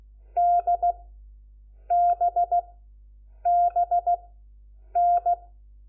Morse code, heard as a keyed tone, about 16 wpm, 700 Hz. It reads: D B B N
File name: Lecciones / CLASE Nº4